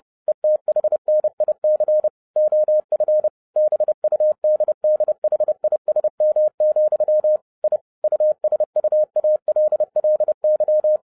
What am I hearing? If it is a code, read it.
Morse code, heard as a keyed tone, about 30 wpm, 610 Hz. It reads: ETHNIC OF BUDDHISM, I USUALLY
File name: 2011-07-mp3 / n60 BACK FROM A BIG ACCIDENT 30wpm